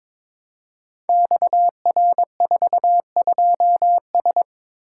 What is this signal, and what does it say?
Morse code, heard as a keyed tone, about 22 wpm, 700 Hz. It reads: XR42S
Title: Code Groups - 5 Letter:Numbers - 22wpm